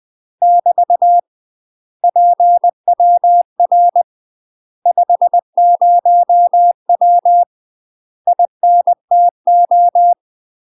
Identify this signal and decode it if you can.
Morse code, heard as a keyed tone, about 20 wpm, 700 Hz. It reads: <BT> PWR 50W INTO
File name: UK MASTER QSOs TRAINING/20WPM